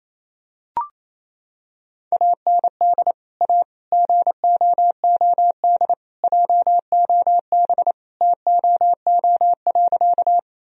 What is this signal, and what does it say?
Morse code, heard as a keyed tone, about 27 wpm, 700 Hz. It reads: AND A GOOD JOB TOO.
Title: Sentences from Top 1000 Words - Review Part 3 - 28wpm